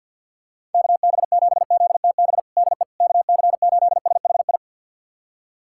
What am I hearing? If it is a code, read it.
Morse code, heard as a keyed tone, about 46 wpm, 700 Hz. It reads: KB7ZTB DE KC8SHI
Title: USA Callsign DE Callsign - 6 Characters - 50wpm